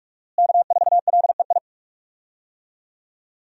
Morse code, heard as a keyed tone, about 46 wpm, 700 Hz. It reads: K4LEI
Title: USA Call Signs - 5 Characters - Repeated 3x - 45wpm